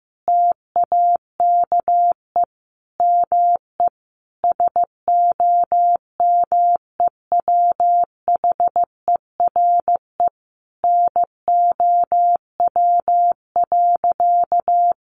Morse code, heard as a keyed tone, about 15 wpm, 700 Hz. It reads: TAKE ME SOMEWHERE NOW.
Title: Sentences from Top 1400 Words 15wpm